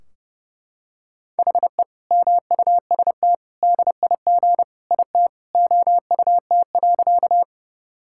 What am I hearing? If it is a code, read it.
Morse code, heard as a keyed tone, about 30 wpm, 700 Hz. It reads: HE MUST DIG IT OUT.
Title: Sentences from Top 1600 Words 30wpm